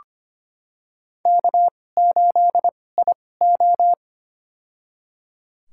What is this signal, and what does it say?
Morse code, heard as a keyed tone, about 25 wpm, spaced as for 18 wpm, 700 Hz. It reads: K8IO
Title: USA Call Signs - 4 Characters - 2x Character Spacing - 25wpm